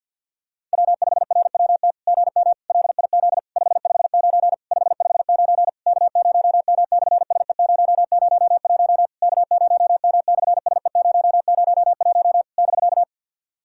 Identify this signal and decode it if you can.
Morse code, heard as a keyed tone, about 49 wpm, 700 Hz. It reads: W5RWT GM LIZ 559 559 K0M/SE001 K0M/SE001 <BK>
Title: SOTA QSO Practice 50wpm